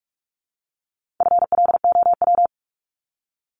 Morse code, heard as a keyed tone, about 44 wpm, 700 Hz. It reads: FLOW
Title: Top 1001-1250 Words 45wpm